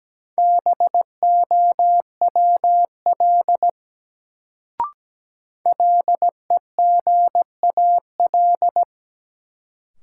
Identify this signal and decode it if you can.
Morse code, heard as a keyed tone, about 17 wpm, 700 Hz. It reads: BOWL LEGAL
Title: Sentences from Top 1200 Words - New Word Drill - 17wpm